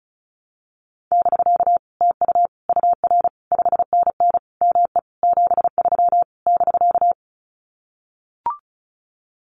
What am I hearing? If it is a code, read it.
Morse code, heard as a keyed tone, about 34 wpm, 700 Hz. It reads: <BK> TU UR 5NN ME 73 <BK>
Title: Common QSO Phrases Encore 35wpm